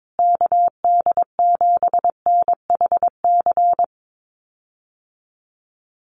Morse code, heard as a keyed tone, about 22 wpm, 700 Hz. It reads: KD7NHC
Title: USA Call Signs Repeated 3x 22wpm